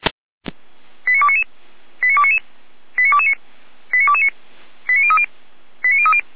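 Five-tone selective call (ZVEI), heard as quick runs of five tones, tones of about 70 ms. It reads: 89299 89299 89229 89229 89030 89030
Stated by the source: ZVEI five-tone call received on SDR, from feuerwehr-orlamuende.org (fire brigade Orlamünde) audio archive.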